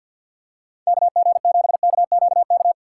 Morse code, heard as a keyed tone, about 51 wpm, 700 Hz. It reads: KG7KQK
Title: USA Call Signs - 3x Character Spacing - 50wpm